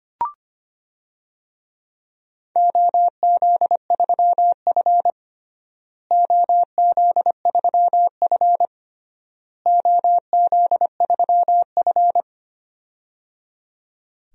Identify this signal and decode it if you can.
Morse code, heard as a keyed tone, about 25 wpm, 700 Hz. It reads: OZ3F OZ3F OZ3F
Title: DXCC-100 Call Signs - 4 Characters - Repeated 3x - 25wpm